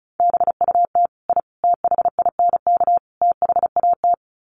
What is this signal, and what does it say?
Morse code, heard as a keyed tone, about 34 wpm, 700 Hz. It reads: BUT I THINK THAT
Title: Sets of 4 Words 35wpm